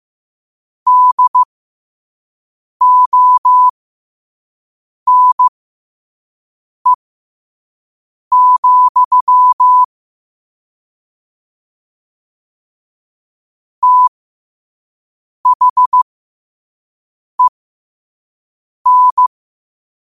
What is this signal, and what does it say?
Morse code, heard as a keyed tone, about 15 wpm, spaced as for 5 wpm, 1000 Hz. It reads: DONE, THEN
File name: qotdfiles / 5wpm